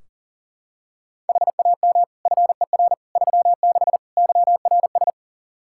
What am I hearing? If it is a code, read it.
Morse code, heard as a keyed tone, about 40 wpm, 700 Hz. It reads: HAM FER 36 YRS
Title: K1USN SST QSO Practice 40wpm